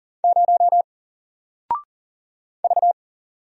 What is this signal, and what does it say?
Morse code, heard as a keyed tone, about 40 wpm, 700 Hz. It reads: 0 V
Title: Single Character (Letter:Number:Punctuation:Prosign) Alt 40wpm